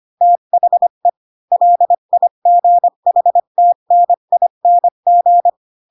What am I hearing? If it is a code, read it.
Morse code, heard as a keyed tone, about 25 wpm, 700 Hz. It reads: THE LIGHTNING
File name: Pinocchio / Pinocchio ch7 20 25 ik1hge